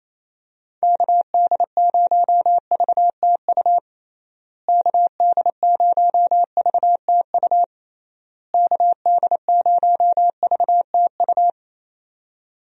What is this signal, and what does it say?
Morse code, heard as a keyed tone, about 27 wpm, 700 Hz. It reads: KD0VTU KD0VTU KD0VTU
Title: USA Call Signs - 6 Characters - Repeated 3x - 28wpm